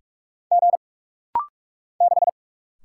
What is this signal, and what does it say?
Morse code, heard as a keyed tone, about 46 wpm, 700 Hz. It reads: G 6
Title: Single Letter-Number - Rapid-Fire 45wpm